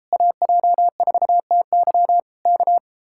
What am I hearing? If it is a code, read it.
Morse code, heard as a keyed tone, about 34 wpm, 700 Hz. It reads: AJ4TY K